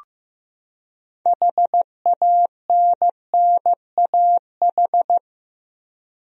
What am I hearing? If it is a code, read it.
Morse code, heard as a keyed tone, about 16 wpm, 700 Hz. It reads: HANNAH
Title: USA Common Names - 15wpm